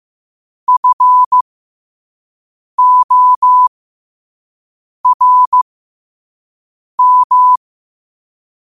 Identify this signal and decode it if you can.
Morse code, heard as a keyed tone, about 15 wpm, spaced as for 5 wpm, 1000 Hz. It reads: FORM
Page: New quotes every day in morse code at 5 Words per minute.